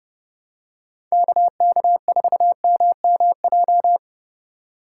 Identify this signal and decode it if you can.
Morse code, heard as a keyed tone, about 32 wpm, 700 Hz. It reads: KK4MMJ
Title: USA Call Signs - 6 Characters - 4x Character Spacing - 30wpm